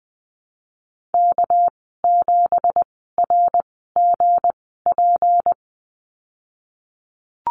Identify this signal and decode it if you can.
Morse code, heard as a keyed tone, about 20 wpm, 700 Hz. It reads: K 7 R G P
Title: 207 Lesson 33g - Letter j - Call Signs 20wpm